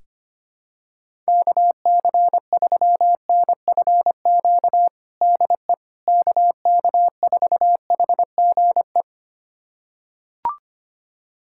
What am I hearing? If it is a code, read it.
Morse code, heard as a keyed tone, about 25 wpm, 700 Hz. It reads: KC3NFQ DE KK4HGE
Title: USA Callsign DE Callsign - 6 Characters - 25wpm